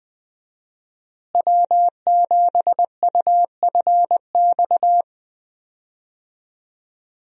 Morse code, heard as a keyed tone, about 20 wpm, 700 Hz. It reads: W7UFX
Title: USA Call Signs - 5 Characters - Repeated 2x - 20wpm